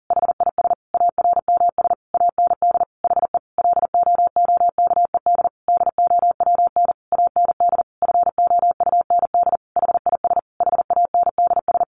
Morse code, heard as a keyed tone, about 40 wpm, 700 Hz. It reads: HIS ARMS AND HE LOOKED DOWN AND FOUND HIS HANDS
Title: A Sound Of Thunder 40wpm